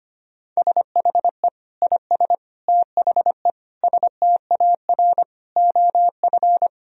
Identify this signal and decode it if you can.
Morse code, heard as a keyed tone, about 25 wpm, 700 Hz. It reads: SHE IS THE STAR OF
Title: Sentences from Top 1000 Words - Review Part 2 - 25wpm